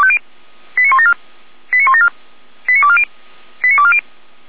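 Five-tone selective call (ZVEI), heard as quick runs of five tones, tones of about 70 ms. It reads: 89174 89174 89240 89240
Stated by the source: ZVEI five-tone call received on SDR, fire brigade Orlamünde (feuerwehr-orlamuende) recording.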